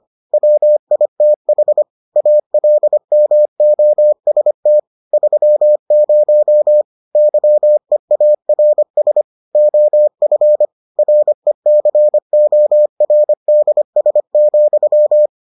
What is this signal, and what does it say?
Morse code, heard as a keyed tone, about 25 wpm, 600 Hz. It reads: WITH ALMOST 30 YEARS OF RECORDS,